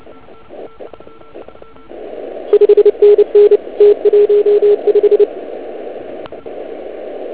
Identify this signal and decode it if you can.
Morse code, heard as a keyed tone, about 28 wpm, 410 Hz. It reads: 5NN T15